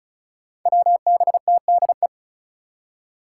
Morse code, heard as a keyed tone, about 34 wpm, 700 Hz. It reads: WBTDE
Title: Five Challenging Letter-Number Combinations 35wpm